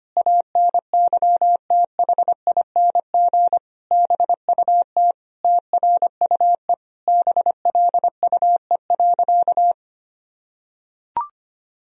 Morse code, heard as a keyed tone, about 25 wpm, 700 Hz. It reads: ANYTHING BUT TRUE BLUE.
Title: Sentences from Top 1000 Words - Review Part 3 - 25wpm